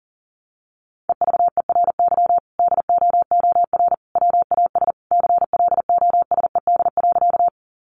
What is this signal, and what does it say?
Morse code, heard as a keyed tone, about 37 wpm, 700 Hz. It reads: EVERY DOOR WAS CLOSED.
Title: Sentences from Top 1400 Words 40wpm